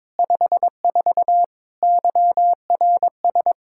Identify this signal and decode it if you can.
Morse code, heard as a keyed tone, about 22 wpm, 700 Hz. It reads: 54 YRS
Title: K1USN SST QSO Practice 22wpm